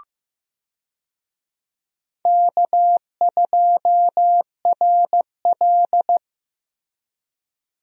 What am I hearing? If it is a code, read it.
Morse code, heard as a keyed tone, about 15 wpm, 700 Hz. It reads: K2RL
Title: USA Call Signs - 4 Characters - Repeated 2x - 15wpm